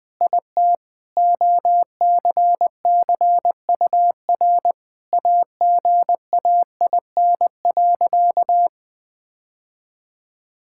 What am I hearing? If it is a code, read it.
Morse code, heard as a keyed tone, about 20 wpm, 700 Hz. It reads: IT OCCUR AGAIN.
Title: Sentences from Top 1000 Words - Review Part 6 - 20wpm